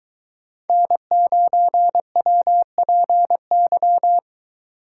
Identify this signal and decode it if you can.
Morse code, heard as a keyed tone, about 22 wpm, 700 Hz. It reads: N9WPY